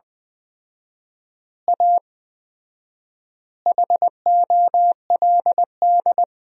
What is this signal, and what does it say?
Morse code, heard as a keyed tone, about 21 wpm, 700 Hz. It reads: A HOLD
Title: 4x Word-Spacing - Sentences from Top 400 Words - 20wpm